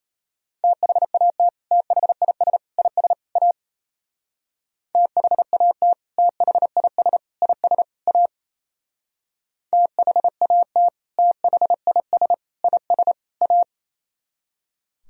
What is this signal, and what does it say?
Morse code, heard as a keyed tone, about 32 wpm, 700 Hz. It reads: THAT THIS IS A THAT THIS IS A THAT THIS IS A
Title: Speed-Racing - Sets of 4 Words 28wpm